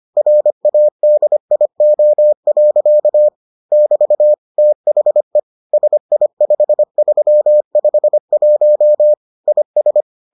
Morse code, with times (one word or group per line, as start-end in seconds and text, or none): 0.2-3.3 RADIO.
3.7-4.3 <BT>
4.6-5.4 THE
5.7-9.1 SI5351
9.5-10.0 IS